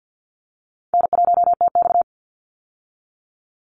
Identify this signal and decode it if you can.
Morse code, heard as a keyed tone, about 50 wpm, 700 Hz. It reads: N1TX
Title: Easy Call Signs Repeated 2x - 50wpm